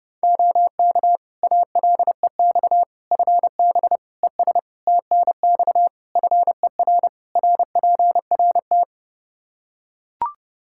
Morse code, heard as a keyed tone, about 30 wpm, 700 Hz. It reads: OK ALEX FB ES TNX FER RPRT
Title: Practice QSO - Exchange 2 - Info - 30wpm